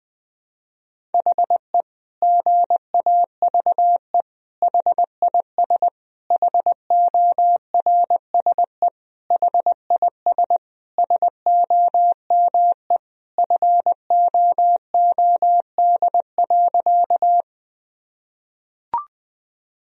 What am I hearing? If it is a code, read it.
Morse code, heard as a keyed tone, about 20 wpm, 700 Hz. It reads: HE GAVE HIS HORSE HIS SOME FOOD.
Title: Sentences from Top 1000 Words - Review Part 1 - 20wpm